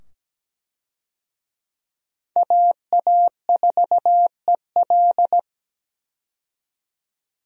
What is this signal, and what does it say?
Morse code, heard as a keyed tone, about 17 wpm, 700 Hz. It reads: AA4EL
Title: USA Call Signs - 5 Characters - 17wpm